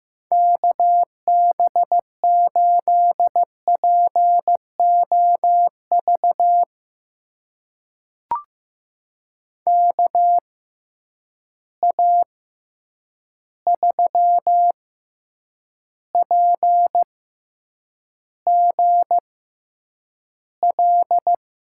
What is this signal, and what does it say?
Morse code, heard as a keyed tone, about 15 wpm, spaced as for 8 wpm, 700 Hz. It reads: KB8POV KA3PGL
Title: USA Call Signs - 6 Characters - 6x Character Spacing - 15wpm